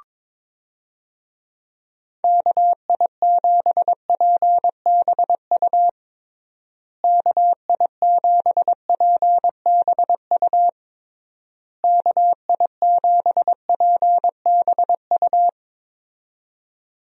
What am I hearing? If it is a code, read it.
Morse code, heard as a keyed tone, about 22 wpm, 700 Hz. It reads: KI7PBU KI7PBU KI7PBU
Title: USA Call Signs - 6 Characters - Repeated 3x - 22wpm